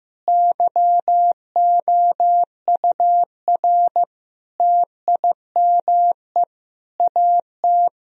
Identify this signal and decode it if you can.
Morse code, heard as a keyed tone, about 15 wpm, 700 Hz. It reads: YOUR TIME AT
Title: Sentences from Top 1000 Words - Review Part 1 - 15wpm